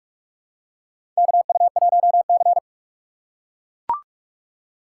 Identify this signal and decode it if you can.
Morse code, heard as a keyed tone, about 43 wpm, 700 Hz. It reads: KU1C
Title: USA Call Signs - 4 Characters - 4x Character Spacing - 45wpm